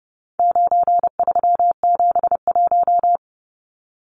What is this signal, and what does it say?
Morse code, heard as a keyed tone, about 30 wpm, 700 Hz. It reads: 9371
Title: Numbers - 4-Digit 30wpm